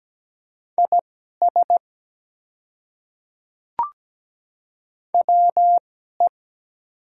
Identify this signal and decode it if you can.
Morse code, heard as a keyed tone, about 17 wpm, spaced as for 12 wpm, 690 Hz. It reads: IS WE
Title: 2x Character Spacing - Top 100 Words - 17wpm